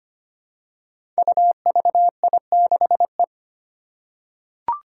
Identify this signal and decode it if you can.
Morse code, heard as a keyed tone, about 25 wpm, 700 Hz. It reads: UVI6E
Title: Code Groups - 5 Letter:Numbers - 25wpm